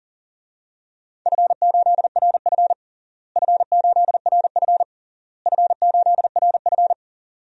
Morse code, heard as a keyed tone, about 37 wpm, 700 Hz. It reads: F8RF F8RF F8RF
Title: DXCC-100 Call Signs - 4 Characters - Repeated 3x - 40wpm